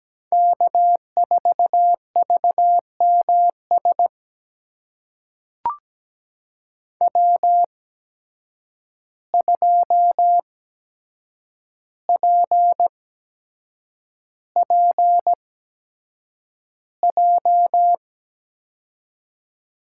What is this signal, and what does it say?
Morse code, heard as a keyed tone, about 17 wpm, spaced as for 7 wpm, 700 Hz. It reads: K4VMS W2PPJ
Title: USA Call Signs - 5 Characters - 8x Character Spacing - 17wpm